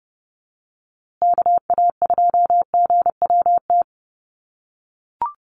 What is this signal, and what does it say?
Morse code, heard as a keyed tone, about 30 wpm, 700 Hz. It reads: KA2GWT
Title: USA Call Signs Repeated 2x 30wpm